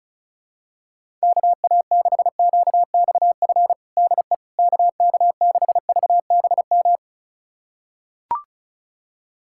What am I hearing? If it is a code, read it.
Morse code, heard as a keyed tone, about 36 wpm, 700 Hz. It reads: KA6QXF DE KK6VBM
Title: USA Callsign DE Callsign - 6 Characters - 35wpm